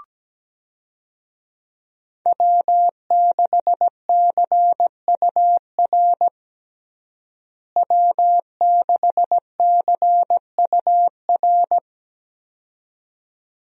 Morse code, heard as a keyed tone, about 17 wpm, 700 Hz. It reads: W6CUR W6CUR
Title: USA Call Signs - 5 Characters - Repeated 2x - 17wpm